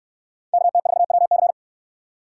Morse code, heard as a keyed tone, about 69 wpm, 700 Hz. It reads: KT4WC